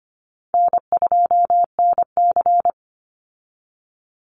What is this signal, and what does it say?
Morse code, heard as a keyed tone, about 25 wpm, 700 Hz. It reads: N2NC
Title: Call Signs Repeated 2x - 25wpm